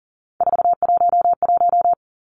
Morse code, heard as a keyed tone, about 38 wpm, 700 Hz. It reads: 411
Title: Three Challenging Letter-Number Combinations 40wpm